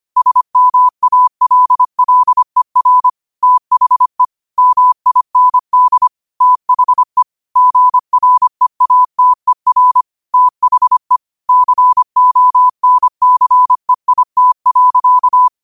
Morse code, heard as a keyed tone, about 25 wpm, 1000 Hz. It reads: SMALLER THE MIND THE GREATER THE CONCEIT.
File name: qotdfiles / 25wpm